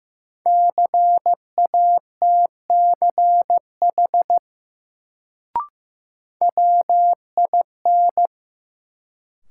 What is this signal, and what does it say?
Morse code, heard as a keyed tone, about 15 wpm, 700 Hz. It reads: CATCH WIN
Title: Sentences from Top 1500 Words - New Word Drill Alt - 15wpm